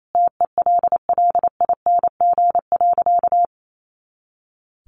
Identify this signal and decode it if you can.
Morse code, heard as a keyed tone, about 28 wpm, 700 Hz. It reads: TELLING.
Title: A Sound Of Thunder 28wpm